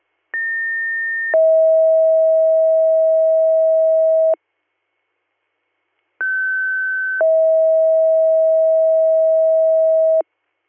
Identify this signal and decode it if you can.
Two-tone page, heard as a one-second tone, then a three-second tone, about 1800 then 640 Hz, about 1550 then 640 Hz. It is heard from 0.3 to 10.2 s.